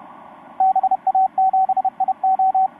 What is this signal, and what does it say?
Morse code, heard as a keyed tone, about 31 wpm, 760 Hz, with radio noise. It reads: BA7IO
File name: vk9-oh1vr ba7io etc 10m cw 22nov2012 0827